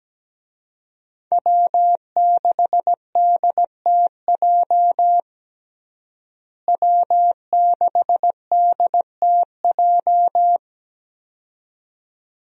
Morse code, heard as a keyed tone, about 17 wpm, 700 Hz. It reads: W6DTJ W6DTJ
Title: USA Call Signs - 5 Characters - Repeated 2x - 17wpm